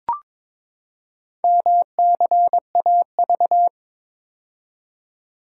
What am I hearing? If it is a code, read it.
Morse code, heard as a keyed tone, about 22 wpm, 700 Hz. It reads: MCAV
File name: practice / 22wpm / Four Letter-Number Combinations - 22wpm